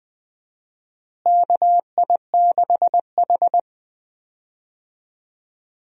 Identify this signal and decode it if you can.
Morse code, heard as a keyed tone, about 20 wpm, 700 Hz. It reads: KI6H
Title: USA Call Signs - 4 Characters - Repeated 3x - 20wpm